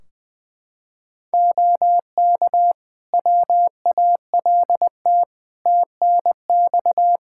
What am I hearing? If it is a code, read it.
Morse code, heard as a keyed tone, about 20 wpm, 700 Hz. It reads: OK WALT TNX
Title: Practice QSO - Exchange 4 - Ending - 20wpm